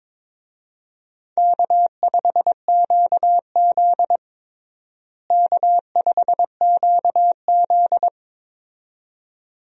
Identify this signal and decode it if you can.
Morse code, heard as a keyed tone, about 22 wpm, 700 Hz. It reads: K5QZ K5QZ
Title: USA Call Signs - 4 Characters - Repeated 2x - 22wpm